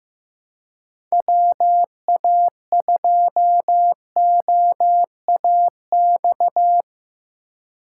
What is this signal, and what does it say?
Morse code, heard as a keyed tone, about 15 wpm, 700 Hz. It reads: WA2OAX
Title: USA Call Signs - 6 Characters - 3x Character Spacing - 15wpm